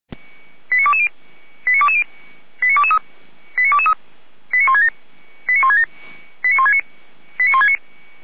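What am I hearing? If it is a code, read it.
Five-tone selective call (ZVEI), heard as quick runs of five tones, tones of about 70 ms. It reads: 89220 89220 89223 89223 89167 89169 89169
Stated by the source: ZVEI five-tone selective call 89163, received with an SDR.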